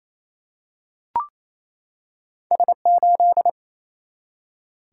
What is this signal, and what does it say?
Morse code, heard as a keyed tone, about 27 wpm, 700 Hz. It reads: S8
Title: Single-Letter Single-Number 28wpm